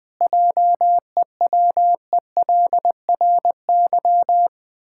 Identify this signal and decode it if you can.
Morse code, heard as a keyed tone, about 20 wpm, 700 Hz. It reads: JEWELRY
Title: Pangram Challenge v2 - 20wpm